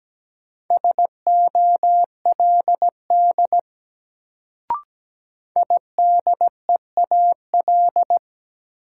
Morse code, heard as a keyed tone, about 17 wpm, 700 Hz. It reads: SOLD IDEAL
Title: Sentences from Top 1400 Words - New Word Drill - 17wpm